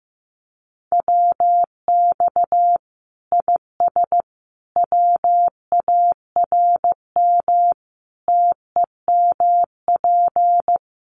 Morse code, heard as a keyed tone, about 15 wpm, 700 Hz. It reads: WX IS WARM TEMP
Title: Common QSO Phrases 15wpm